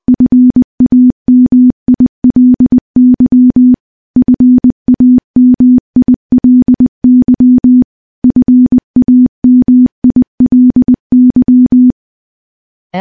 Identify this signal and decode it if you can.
Morse code, heard as a keyed tone, about 20 wpm, 250 Hz. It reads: FAMILY FAMILY FAMILY
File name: morse / morse-training-20WPM